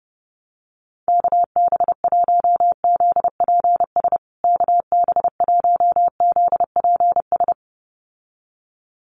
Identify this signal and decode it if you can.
Morse code, heard as a keyed tone, about 30 wpm, 700 Hz. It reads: KB1ZPS KB1ZPS
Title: USA Call Signs Repeated 2x 30wpm